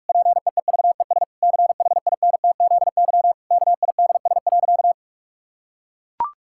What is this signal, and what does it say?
Morse code, heard as a keyed tone, about 41 wpm, 700 Hz. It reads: PEEVES CHINTZY KIDS.
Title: Pangram Challenge v2 - 45wpm